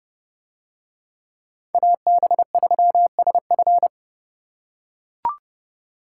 Morse code, heard as a keyed tone, about 29 wpm, 700 Hz. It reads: AB3SF